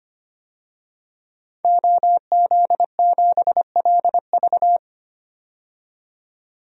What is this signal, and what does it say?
Morse code, heard as a keyed tone, about 25 wpm, 700 Hz. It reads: OZ7LV
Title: DXCC-100 Call Signs - 5 Characters - 25wpm